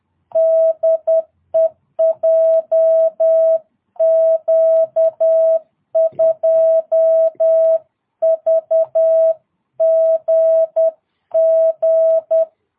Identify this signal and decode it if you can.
Morse code, heard as a keyed tone, about 10 wpm, 660 Hz. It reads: DE JQ2VGG
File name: jh8cbh / 2022cw